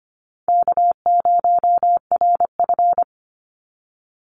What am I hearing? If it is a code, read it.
Morse code, heard as a keyed tone, about 25 wpm, 700 Hz. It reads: K0RF
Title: Easy Call Signs Repeated 3x - 25wpm